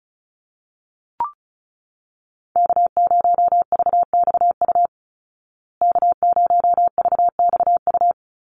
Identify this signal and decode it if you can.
Morse code, heard as a keyed tone, about 34 wpm, 700 Hz. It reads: K0VXU K0VXU
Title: Call Signs Repeated 3x - 35wpm